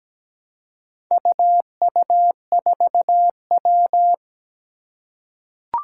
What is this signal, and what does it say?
Morse code, heard as a keyed tone, about 17 wpm, 700 Hz. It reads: UU4W
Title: Four Challenging Letter-Number Combinations 17wpm